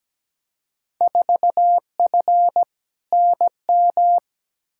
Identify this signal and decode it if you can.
Morse code, heard as a keyed tone, about 17 wpm, 700 Hz. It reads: 4F NM
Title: Field Day Exchanges Encore - 17wpm